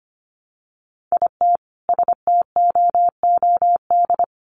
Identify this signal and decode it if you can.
Morse code, heard as a keyed tone, about 25 wpm, 700 Hz. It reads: IT STOOD
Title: Sentences from Top 500 Words 25wpm